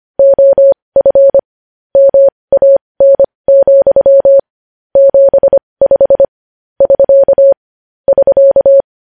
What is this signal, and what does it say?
Morse code, heard as a keyed tone, about 25 wpm, spaced as for 20 wpm, 550 Hz. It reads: OF MAN, 75 <SK> <SK>
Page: news-headlines-in-morse-code-at-20-wpm